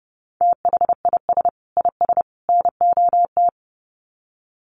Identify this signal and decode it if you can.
Morse code, heard as a keyed tone, about 28 wpm, 700 Hz. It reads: THIS IS NOT
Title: Sets of 3 Words 30wpm